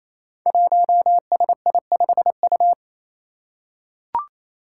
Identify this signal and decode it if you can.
Morse code, heard as a keyed tone, about 28 wpm, 700 Hz. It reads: 1SI5U
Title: Five Challenging Letter-Number Combinations 28wpm